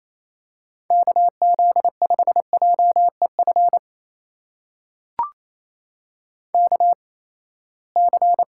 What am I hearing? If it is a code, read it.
Morse code, heard as a keyed tone, about 28 wpm, spaced as for 19 wpm, 700 Hz. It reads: KZ5JEF KC
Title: USA Call Signs - 6 Characters - 8x Character Spacing - 28wpm